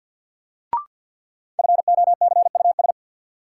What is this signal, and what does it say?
Morse code, heard as a keyed tone, about 51 wpm, 700 Hz. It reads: FOCUS
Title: Sentences from Top 1000 Words - Review Part 4 - New Word Drill - 50wpm